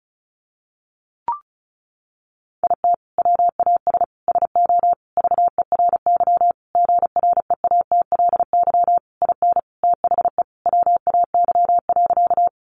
Morse code, heard as a keyed tone, about 34 wpm, 700 Hz. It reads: IT WAS SO VERY GREATLY IN THE WAY.
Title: Sentences from Top 1400 Words 35wpm